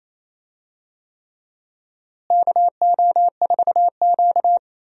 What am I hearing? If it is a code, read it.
Morse code, heard as a keyed tone, about 28 wpm, 700 Hz. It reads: KO4Q
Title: USA Call Signs - 4 Characters - Repeated 3x - 28wpm